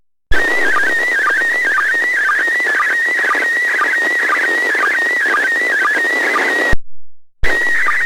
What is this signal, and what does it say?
SSTV (slow-scan television), PD 120.
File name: posts / receiving-sstv-images-from-iss / iss sstv2